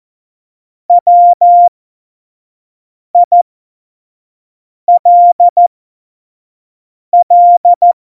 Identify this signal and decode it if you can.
Morse code, heard as a keyed tone, about 14 wpm, spaced as for 5 wpm, 700 Hz. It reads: WILL
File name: Morsekurs / Witze / Witz 03. Oktober 17, Tempo 25 F70